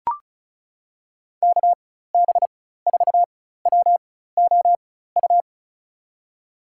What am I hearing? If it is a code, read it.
Morse code, heard as a keyed tone, about 36 wpm, spaced as for 16 wpm, 700 Hz. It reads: KB4WOU
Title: USA Call Signs - 4x Character Spacing - 35wpm